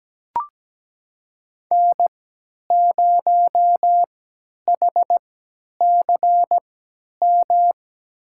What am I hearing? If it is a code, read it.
Morse code, heard as a keyed tone, about 17 wpm, spaced as for 10 wpm, 700 Hz. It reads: N0HCM
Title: USA Call Signs - 5 Characters - 3x Character Spacing - 17wpm